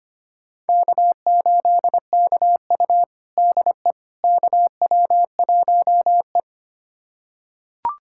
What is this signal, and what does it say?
Morse code, heard as a keyed tone, about 25 wpm, 700 Hz. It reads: K8KU DE KW1E
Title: USA Callsign DE Callsign - 4 Characters - 25wpm